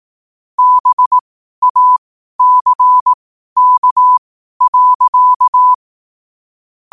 Morse code, heard as a keyed tone, about 18 wpm, spaced as for 12 wpm, 1000 Hz. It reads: BACK.
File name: qotdfiles / 12wpm